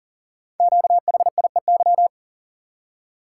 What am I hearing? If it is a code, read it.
Morse code, heard as a keyed tone, about 38 wpm, 700 Hz. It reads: QHIEY
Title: Code Groups - 5 Letters - 40wpm